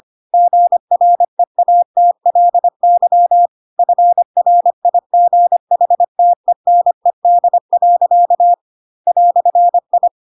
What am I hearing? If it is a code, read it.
Morse code, heard as a keyed tone, about 25 wpm, 700 Hz. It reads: GREATLY FRIGHTENED. "I
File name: Pinocchio / Pinocchio ch4 25 25 ik1hge